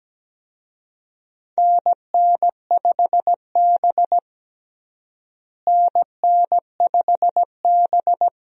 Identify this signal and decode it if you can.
Morse code, heard as a keyed tone, about 17 wpm, 700 Hz. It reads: NN5B NN5B
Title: USA Call Signs - 4 Characters - Repeated 2x - 17wpm